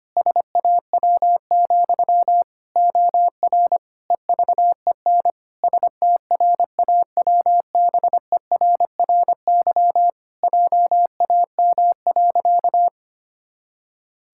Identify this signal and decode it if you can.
Morse code, heard as a keyed tone, about 25 wpm, 700 Hz. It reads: SAW, OR EVEN STRAWBERRY JAM.
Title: Pangram Challenge v2 - 25wpm